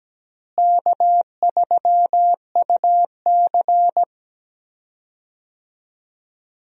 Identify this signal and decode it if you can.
Morse code, heard as a keyed tone, about 17 wpm, 700 Hz. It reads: K3UC
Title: USA Call Signs - 4 Characters - Repeated 2x - 17wpm